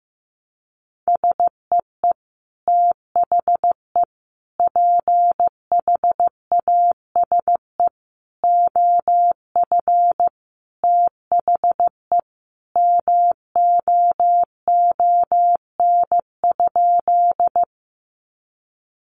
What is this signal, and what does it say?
Morse code, heard as a keyed tone, about 15 wpm, 700 Hz. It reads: SEE THE PHASE OF THE MOON?
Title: Sentences from Top 1200 Words 15wpm